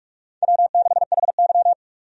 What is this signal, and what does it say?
Morse code, heard as a keyed tone, about 45 wpm, 700 Hz. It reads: W6HY
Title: USA Call Signs - 4 Characters - 45wpm